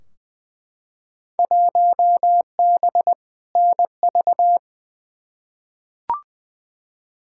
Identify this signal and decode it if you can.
Morse code, heard as a keyed tone, about 20 wpm, 700 Hz. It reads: 1B NV
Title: Field Day Exchanges - 20wpm